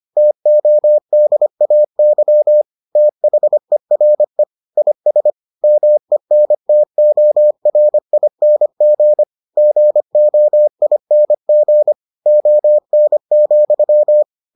Morse code, heard as a keyed tone, about 25 wpm, 600 Hz. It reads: TODAY THERE IS MENTORING GOING ON,